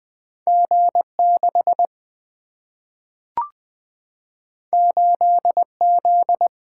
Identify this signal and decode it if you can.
Morse code, heard as a keyed tone, about 20 wpm, 700 Hz. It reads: G6 8Z
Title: Single-Letter Single-Number 20wpm